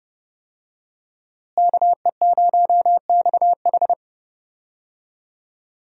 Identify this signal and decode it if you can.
Morse code, heard as a keyed tone, about 30 wpm, 700 Hz. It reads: KE0XH
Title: USA Call Signs - 5 Characters - Repeated 2x - 30wpm